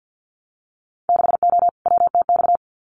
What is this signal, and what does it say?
Morse code, heard as a keyed tone, about 50 wpm, 700 Hz. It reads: 6O WTX
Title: Winter Field Day Exchanges Encore 50wpm